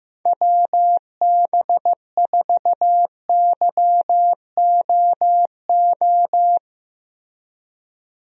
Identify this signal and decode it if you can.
Morse code, heard as a keyed tone, about 15 wpm, 700 Hz. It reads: WB4YOO
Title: USA Call Signs - 6 Characters - Repeated 3x - 15wpm